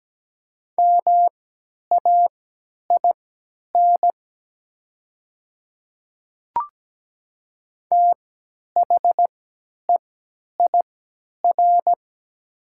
Morse code, heard as a keyed tone, about 17 wpm, spaced as for 10 wpm, 700 Hz. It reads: MAIN THEIR
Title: 3x Character Spacing - Top 300 Words - 17wpm